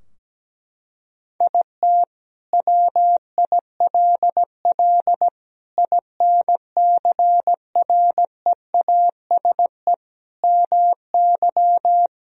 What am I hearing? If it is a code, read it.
Morse code, heard as a keyed tone, about 17 wpm, 700 Hz. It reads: IT WILL INCREASE MY
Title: Sentences from Top 1000 Words - Review Part 4 - 17wpm